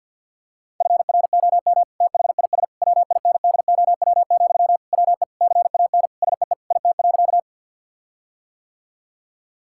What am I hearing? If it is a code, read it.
Morse code, heard as a keyed tone, about 50 wpm, 700 Hz. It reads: FROM THIS WINDOW, WE CAN SEE IT.
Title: Sentences from Top 1600 Words 50wpm